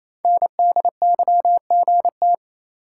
Morse code, heard as a keyed tone, about 27 wpm, 700 Hz. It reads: NDYGT
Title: Code Groups - 5 Letter:Numbers - 28wpm